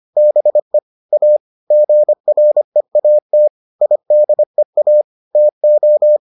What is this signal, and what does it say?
Morse code, heard as a keyed tone, about 25 wpm, 600 Hz. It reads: BE A GREAT IDEA TO